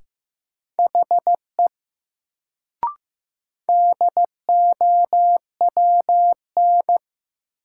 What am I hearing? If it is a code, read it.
Morse code, heard as a keyed tone, about 15 wpm, 700 Hz. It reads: HE DOWN
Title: Sets of 2 Words - New Word Drill - 15wpm